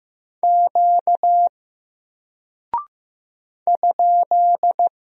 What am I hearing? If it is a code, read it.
Morse code, heard as a keyed tone, about 15 wpm, 700 Hz. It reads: Q ?
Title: Single Character (Letter:Number:Punctuation:Prosign) Alt 15wpm